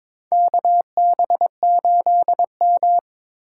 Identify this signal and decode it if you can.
Morse code, heard as a keyed tone, about 21 wpm, 700 Hz. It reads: KB8M
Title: K1USN SST QSO Practice - 22wpm